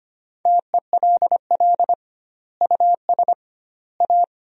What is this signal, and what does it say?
Morse code, heard as a keyed tone, about 25 wpm, 700 Hz. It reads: TELL US A
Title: 2x Word-Spacing - Sentences from Top 300 Words - 25wpm